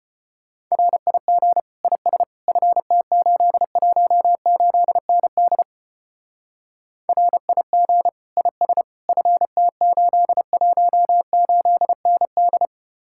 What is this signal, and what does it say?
Morse code, heard as a keyed tone, about 31 wpm, 700 Hz. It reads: RIG IS FT818ND RIG IS FT818ND
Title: Speed-Racing Common QSO Phrases Encore - 25wpm